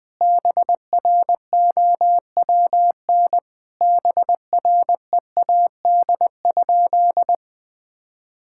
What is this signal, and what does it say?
Morse code, heard as a keyed tone, about 21 wpm, 700 Hz. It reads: BROWN BREAD?
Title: Sentences from Top 1600 Words 20wpm